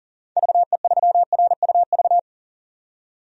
Morse code, heard as a keyed tone, about 40 wpm, 700 Hz. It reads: VE3RUV
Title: DXCC-100 Call Signs - 6 Characters - 40wpm